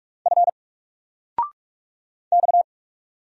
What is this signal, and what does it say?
Morse code, heard as a keyed tone, about 44 wpm, 700 Hz. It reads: F X
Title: Single Character (Letter:Number:Punctuation:Prosign) 45wpm